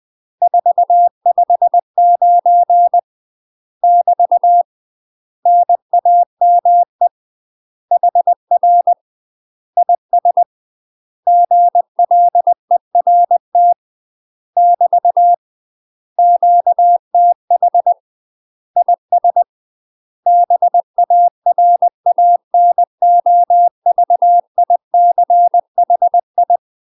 Morse code, heard as a keyed tone, about 20 wpm, 700 Hz. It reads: 459 <BT> NAME HR IS GLERT <BT> QTH IS BARANOVICHI
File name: UK MASTER QSOs TRAINING / 20WPM / UK MASTER QSOs 0015